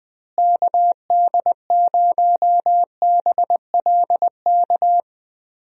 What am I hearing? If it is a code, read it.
Morse code, heard as a keyed tone, about 21 wpm, 700 Hz. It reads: KD0BLK
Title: USA Call Signs - 6 Characters - Repeated 3x - 20wpm